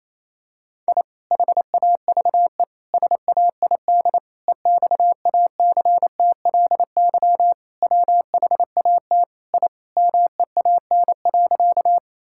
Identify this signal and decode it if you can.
Morse code, heard as a keyed tone, about 27 wpm, 700 Hz. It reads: I HAVE SAID EXACTLY WHAT I MEAN.
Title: Sentences from Top 1000 Words - Review Part 6 - 28wpm